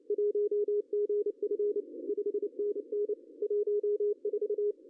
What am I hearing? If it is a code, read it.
Morse code, heard as a keyed tone, about 28 wpm, 410 Hz. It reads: 1GF 5NN 14
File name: CW / cqww-cw-k5tr